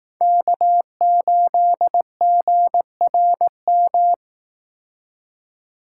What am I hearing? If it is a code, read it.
Morse code, heard as a keyed tone, about 19 wpm, 700 Hz. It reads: K8GRM
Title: Speed Racing - USA 1x3 Call Signs 15wpm